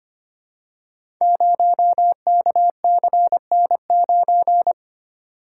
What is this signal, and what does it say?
Morse code, heard as a keyed tone, about 25 wpm, 700 Hz. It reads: 0KCN9
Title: Code Groups - 5 Letter:Numbers - 25wpm